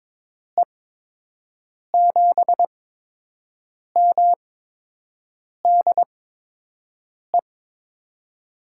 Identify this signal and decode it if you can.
Morse code, heard as a keyed tone, about 22 wpm, spaced as for 6 wpm, 700 Hz. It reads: E7MDE
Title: USA Call Signs - 6 Characters - 8x Character Spacing - 22wpm